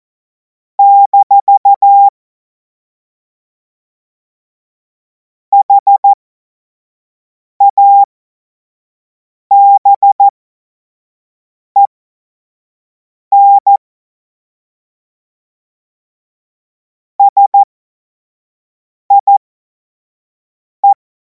Morse code, heard as a keyed tone, about 14 wpm, spaced as for 5 wpm, 800 Hz. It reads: - HABEN SIE
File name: Morsekurs / Witze / Witz 03. März 2026 25 F70